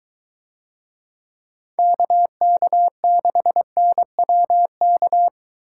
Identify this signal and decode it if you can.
Morse code, heard as a keyed tone, about 22 wpm, 700 Hz. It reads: KK6NWK